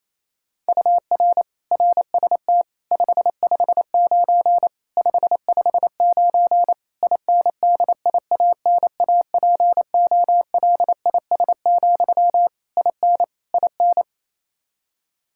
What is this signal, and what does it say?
Morse code, heard as a keyed tone, about 27 wpm, 700 Hz. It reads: UR RST 559 559 INDIANAPOLIS, IN IN
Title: Practice QSO - Exchange 1 - Beginning - 28wpm